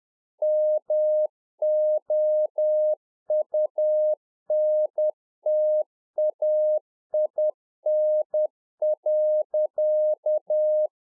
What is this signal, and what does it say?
Morse code, heard as a keyed tone, about 10 wpm, 610 Hz. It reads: MOUNTAIN.
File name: 2011-04-mp3 / RICK JJ0MPI 10WPM